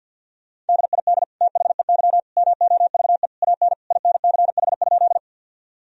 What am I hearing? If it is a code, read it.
Morse code, heard as a keyed tone, about 50 wpm, 700 Hz. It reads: DID THEY MOVE AN INCH?
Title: Sentences from Top 1000 Words - Review Part 3 - 50wpm